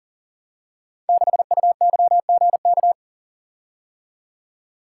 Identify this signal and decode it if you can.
Morse code, heard as a keyed tone, about 42 wpm, 700 Hz. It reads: 6UYGK
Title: Code Groups - 5 Letter:Numbers - 40wpm